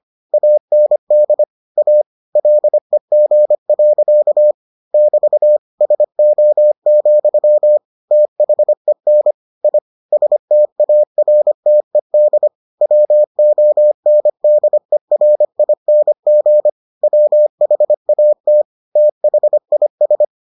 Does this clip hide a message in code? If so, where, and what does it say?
Yes: Morse code from 0.3 to 20.2 s, heard as a keyed tone, about 25 wpm, 600 Hz. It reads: AND A LEG. <BT> SO, THEN I STARTED WONDERING WHAT THIS